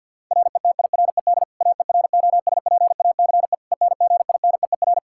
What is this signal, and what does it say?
Morse code, heard as a keyed tone, about 51 wpm, 700 Hz. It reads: RETIRED AEROSPACE ENGINEER
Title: K1USN SST QSO Practice 50wpm